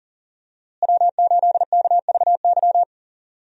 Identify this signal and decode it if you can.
Morse code, heard as a keyed tone, about 40 wpm, 700 Hz. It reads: W8KVY
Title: USA Call Signs - 5 Characters - Repeated 2x - 40wpm